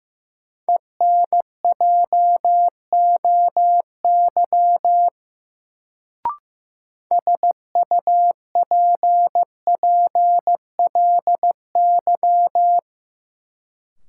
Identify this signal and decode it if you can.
Morse code, heard as a keyed tone, about 15 wpm, 700 Hz. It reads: ENJOY SUPPLY
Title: Sentences from Top 1000 Words - Review Part 4 - New Word Drill - 15wpm